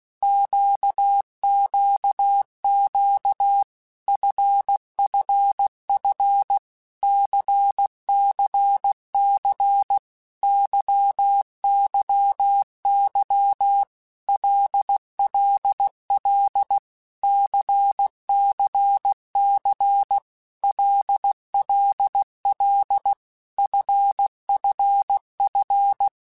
Morse code, heard as a keyed tone, about 16 wpm, 800 Hz. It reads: QQQ FFF CCC YYY LLL CCC LLL FFF
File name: Basic Letter Groups By7Groups / LFCYQ / LFCYQ@15